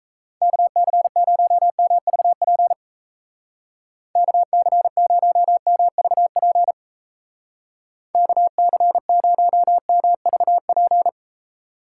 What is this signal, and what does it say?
Morse code, heard as a keyed tone, about 35 wpm, 700 Hz. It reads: KC0MVP KC0MVP KC0MVP
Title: Speed Racing - USA Call Signs 28wpm